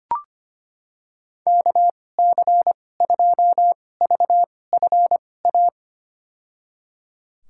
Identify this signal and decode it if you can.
Morse code, heard as a keyed tone, about 25 wpm, spaced as for 18 wpm, 700 Hz. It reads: KC2VFA
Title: USA Call Signs - 6 Characters - 2x Character Spacing - 25wpm